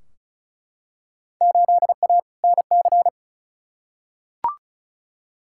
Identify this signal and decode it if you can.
Morse code, heard as a keyed tone, about 34 wpm, 700 Hz. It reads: 8A NC
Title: Field Day Exchanges - 35wpm